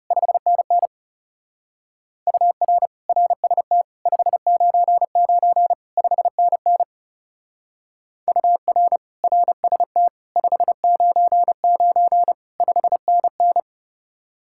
Morse code, heard as a keyed tone, about 37 wpm, 700 Hz. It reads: 5NN UR RST 599 5NN UR RST 599 5NN
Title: Speed-Racing Common QSO Phrases Encore - 30wpm